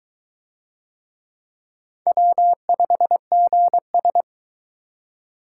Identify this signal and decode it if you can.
Morse code, heard as a keyed tone, about 20 wpm, 700 Hz. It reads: W5GS